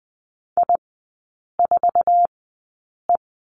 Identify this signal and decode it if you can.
Morse code, heard as a keyed tone, about 20 wpm, spaced as for 8 wpm, 700 Hz. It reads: I4E
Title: 047 Lesson 9e - Number 4 - Three Characters 20wpm